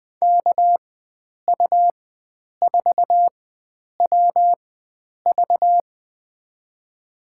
Morse code, heard as a keyed tone, about 20 wpm, spaced as for 9 wpm, 700 Hz. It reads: KU4WV
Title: USA Call Signs - 5 Characters - 4x Character Spacing - 20wpm